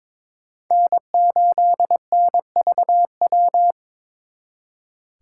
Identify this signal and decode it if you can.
Morse code, heard as a keyed tone, about 26 wpm, 700 Hz. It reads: N8NVW
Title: Speed Racing - USA 1x3 Call Signs 22wpm